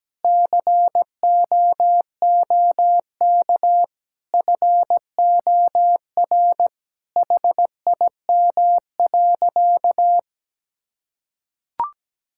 Morse code, heard as a keyed tone, about 17 wpm, 700 Hz. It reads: COOK FOR HIM.
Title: Sentences from Top 1600 Words 17wpm